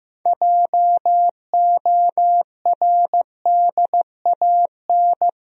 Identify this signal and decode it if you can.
Morse code, heard as a keyed tone, about 15 wpm, 700 Hz. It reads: JORDAN
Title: USA Common Names - 15wpm